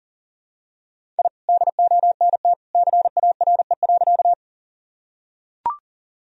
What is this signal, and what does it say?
Morse code, heard as a keyed tone, about 40 wpm, 700 Hz. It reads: I DONT CARE.
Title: Sentences from Top 1000 Words - Review Part 1 - 40wpm